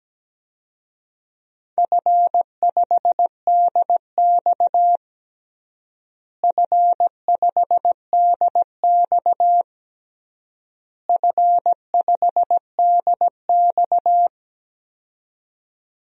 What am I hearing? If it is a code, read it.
Morse code, heard as a keyed tone, about 17 wpm, 700 Hz. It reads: F5DX F5DX F5DX
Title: DXCC-100 Call Signs - 4 Characters - Repeated 3x - 17wpm